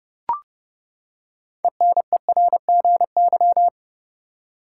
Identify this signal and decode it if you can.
Morse code, heard as a keyed tone, about 30 wpm, 700 Hz. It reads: ENERGY
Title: Six Letter Words - 30wpm